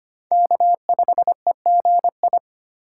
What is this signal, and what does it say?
Morse code, heard as a keyed tone, about 25 wpm, 700 Hz. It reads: K5EGI
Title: USA Call Signs - 5 Characters - 4x Character Spacing - 25wpm